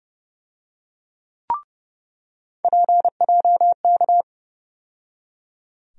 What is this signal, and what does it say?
Morse code, heard as a keyed tone, about 30 wpm, 700 Hz. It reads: PJK
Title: USA Call Sign Postfixes - 3 Letters 30wpm